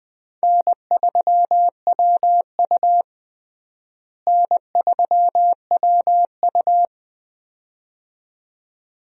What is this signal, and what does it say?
Morse code, heard as a keyed tone, about 21 wpm, 700 Hz. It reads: N3WU N3WU
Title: USA Call Signs - 4 Characters - Repeated 2x - 20wpm